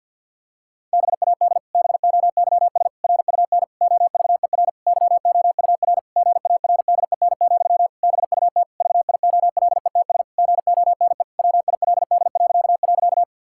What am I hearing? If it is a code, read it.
Morse code, heard as a keyed tone, about 50 wpm, 700 Hz. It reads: BAD BOYS RUN OVER YOUR GARDEN, BUT VIOLETS GONE WILD..
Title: Pangram Challenge v2 - 50wpm